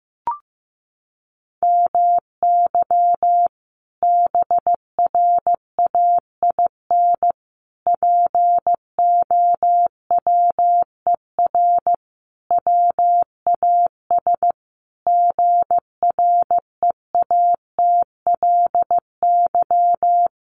Morse code, heard as a keyed tone, about 15 wpm, 700 Hz. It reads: MY BRAIN POWER WAS GREATLY
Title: Sentences from Top 1400 Words 15wpm